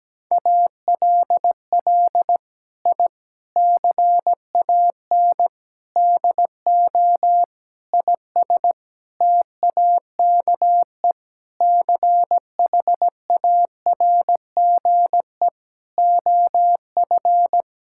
Morse code, heard as a keyed tone, about 17 wpm, 700 Hz. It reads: ALL I CAN DO IS TAKE CHARGE OF
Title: Sentences from Top 1000 Words - Review Part 6 - 17wpm